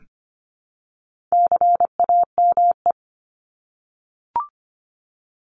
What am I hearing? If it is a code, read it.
Morse code, heard as a keyed tone, about 25 wpm, 700 Hz. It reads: CAME
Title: Top 200 Words Repeated 25wpm